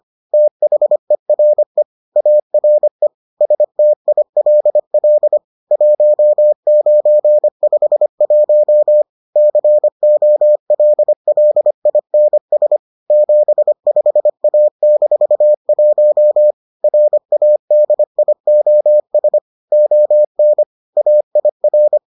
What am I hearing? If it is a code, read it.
Morse code, heard as a keyed tone, about 25 wpm, 600 Hz. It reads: THERE ARE STILL 1951 COLLINS 75A-1 RADIOS ON AIR